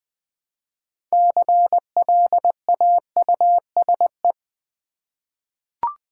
Morse code, heard as a keyed tone, about 20 wpm, 700 Hz. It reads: CLAUSE
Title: 6 Letter Words - First 3 Letters Given - 20wpm